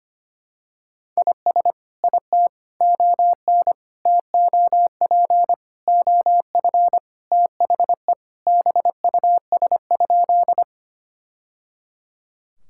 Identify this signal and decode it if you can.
Morse code, heard as a keyed tone, about 25 wpm, 700 Hz. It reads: IS IT ON TOP OF THE BUS?
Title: Sentences from Top 1000 Words - Review Part 6 - 25wpm